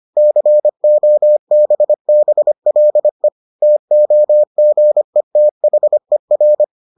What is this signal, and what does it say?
Morse code, heard as a keyed tone, about 25 wpm, 600 Hz. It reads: COBBLE TOGETHER